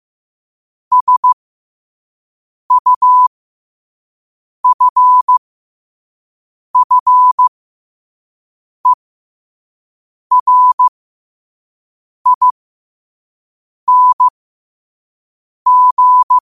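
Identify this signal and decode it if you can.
Morse code, heard as a keyed tone, about 15 wpm, spaced as for 5 wpm, 1000 Hz. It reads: SUFFERING
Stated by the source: Morse code practice at 5 wpm.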